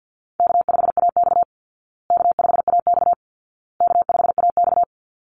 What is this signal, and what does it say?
Morse code, heard as a keyed tone, about 48 wpm, 700 Hz. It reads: K5AX K5AX K5AX
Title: Easy Call Signs Repeated 3x - 50wpm